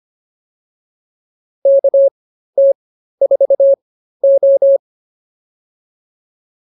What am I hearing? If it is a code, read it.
Morse code, heard as a keyed tone, about 25 wpm, spaced as for 13 wpm, 550 Hz. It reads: KT4O
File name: POTA207 13